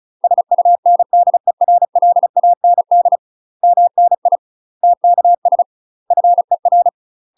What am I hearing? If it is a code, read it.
Morse code, heard as a keyed tone, about 34 wpm, 700 Hz. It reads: SUNDERLAND MNI TKS FER
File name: UK MASTER QSOs TRAINING / 35WPM / UK MASTER QSOs 0003